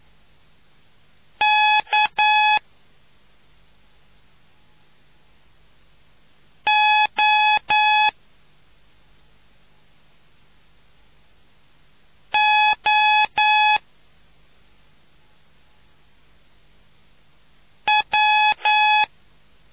Morse code, heard as a keyed tone, about 9 wpm, spaced as for 2 wpm, 850 Hz. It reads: KOOW